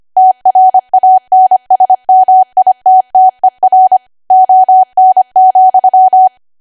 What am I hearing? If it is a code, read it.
Morse code, heard as a keyed tone, about 25 wpm, 750 Hz. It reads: TRANSMITTER ON,
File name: Archive / 25 WPM / 25WPM